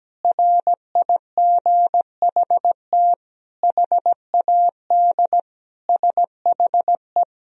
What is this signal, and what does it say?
Morse code, heard as a keyed tone, about 17 wpm, 700 Hz. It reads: RIGHT HAD SHE
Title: Sentences from Top 1000 Words - Review Part 1 - 17wpm